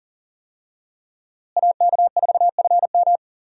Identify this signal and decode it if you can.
Morse code, heard as a keyed tone, about 40 wpm, 700 Hz. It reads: AK4FM
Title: USA Call Signs - 5 Characters - Repeated 2x - 40wpm